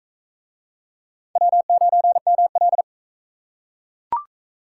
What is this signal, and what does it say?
Morse code, heard as a keyed tone, about 41 wpm, 700 Hz. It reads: W9ML